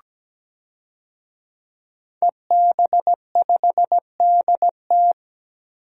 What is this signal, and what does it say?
Morse code, heard as a keyed tone, about 17 wpm, 700 Hz. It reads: EB5DT
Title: DXCC-100 Call Signs - 5 Characters - 17wpm